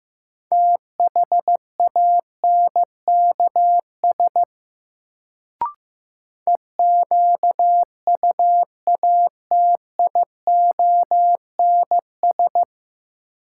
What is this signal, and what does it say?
Morse code, heard as a keyed tone, about 16 wpm, 700 Hz. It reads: THANKS EQUATIONS
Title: Sets of 5 Words - New Word Drill - 15wpm